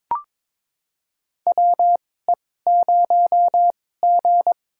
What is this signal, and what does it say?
Morse code, heard as a keyed tone, about 22 wpm, spaced as for 16 wpm, 700 Hz. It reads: WE0G
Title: USA Call Signs - 4 Characters - 2x Character Spacing - 22wpm